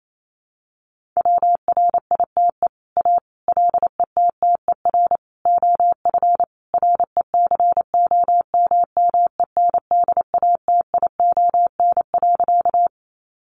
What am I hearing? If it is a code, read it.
Morse code, heard as a keyed tone, about 28 wpm, 700 Hz. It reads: WRITE A LETTER OF RECOMMENDATION.
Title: Sentences from Top 1400 Words 28wpm